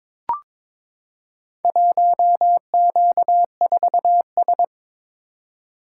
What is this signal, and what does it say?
Morse code, heard as a keyed tone, about 23 wpm, 700 Hz. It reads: 1Q4S
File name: practice / 22wpm / Four Letter-Number Combinations - 22wpm